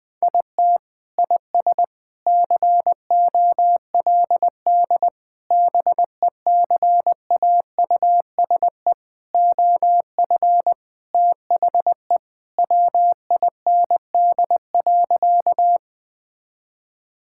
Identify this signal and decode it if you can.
Morse code, heard as a keyed tone, about 20 wpm, 700 Hz. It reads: IT IS COLD BECAUSE OF THE WIND.
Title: Sentences with 3-Word Phrases - 20wpm